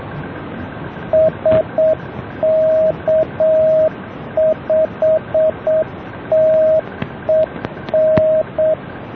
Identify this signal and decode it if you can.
Morse code, heard as a keyed tone, about 7 wpm, 630 Hz, with radio noise. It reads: SK5TEN